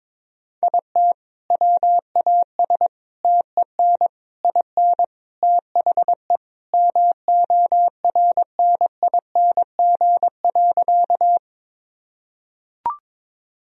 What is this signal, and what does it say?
Morse code, heard as a keyed tone, about 21 wpm, 700 Hz. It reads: IT WAS TEN IN THE MORNING.
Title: Sentences from Top 1000 Words - Review Part 2 - 22wpm